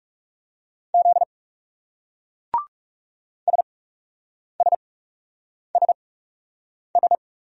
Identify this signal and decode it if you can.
Morse code, heard as a keyed tone, about 38 wpm, spaced as for 8 wpm, 700 Hz. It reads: Z SSSS